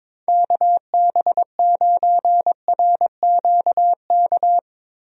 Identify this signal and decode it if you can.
Morse code, heard as a keyed tone, about 22 wpm, 700 Hz. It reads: KB9RQK
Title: USA Call Signs - 6 Characters - Repeated 2x - 22wpm